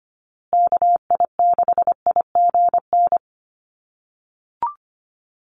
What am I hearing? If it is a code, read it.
Morse code, heard as a keyed tone, about 25 wpm, 700 Hz. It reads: KI6IGN
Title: USA Call Signs Repeated 2x 25wpm